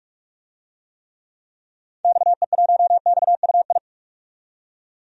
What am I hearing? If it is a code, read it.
Morse code, heard as a keyed tone, about 44 wpm, 690 Hz. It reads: XE1XUI
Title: DXCC-100 Call Signs - 6 Characters - 45wpm